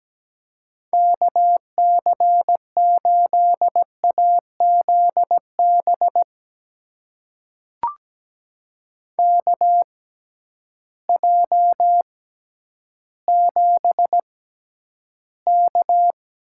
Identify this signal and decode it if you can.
Morse code, heard as a keyed tone, about 17 wpm, spaced as for 10 wpm, 700 Hz. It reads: KC8AZB KJ7K
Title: USA Call Signs - 6x Character Spacing - 17wpm